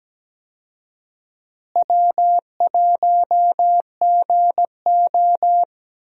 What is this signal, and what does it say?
Morse code, heard as a keyed tone, about 17 wpm, 700 Hz. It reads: W1GO
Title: USA Call Signs - 4 Characters - Repeated 3x - 17wpm